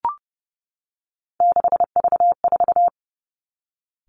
Morse code, heard as a keyed tone, about 30 wpm, 700 Hz. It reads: 6V4
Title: Three Challenging Letter-Number Combinations 30wpm